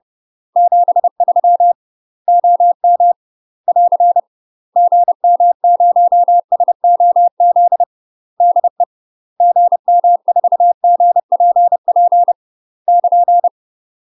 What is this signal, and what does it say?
Morse code, heard as a keyed tone, about 30 wpm, 700 Hz. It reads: 73 OM <AR> GM0SOZ DE GM4GPP <KN>
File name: UK MASTER QSOs TRAINING / 30WPM / UK MASTER QSOs 0007